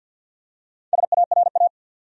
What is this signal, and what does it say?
Morse code, heard as a keyed tone, about 51 wpm, 700 Hz. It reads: SARA
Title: USA Common Names - Max 5 Letters - 50wpm